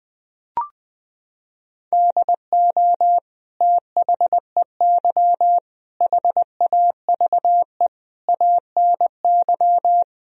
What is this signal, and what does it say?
Morse code, heard as a keyed tone, about 21 wpm, 700 Hz. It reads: DO THEY HAVE ANY
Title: Sentences from Top 1000 Words - Review Part 3 - 20wpm